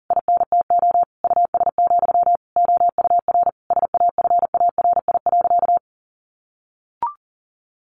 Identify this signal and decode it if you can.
Morse code, heard as a keyed tone, about 40 wpm, 700 Hz. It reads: INTO US, OUR SAFARI.
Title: A Sound Of Thunder 40wpm